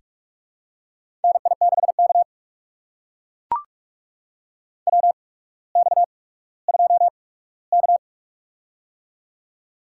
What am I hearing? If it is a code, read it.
Morse code, heard as a keyed tone, about 44 wpm, spaced as for 19 wpm, 700 Hz. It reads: NI6K WX2K
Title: USA Call Signs - 4 Characters - 8x Character Spacing - 45wpm